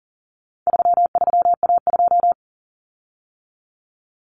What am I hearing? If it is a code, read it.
Morse code, heard as a keyed tone, about 40 wpm, 700 Hz. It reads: 33A2
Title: Common Cut Numbers - 4-Digit 40wpm